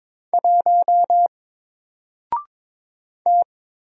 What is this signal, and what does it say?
Morse code, heard as a keyed tone, about 22 wpm, 700 Hz. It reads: 1 T
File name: Single-Letter-Number-22wpm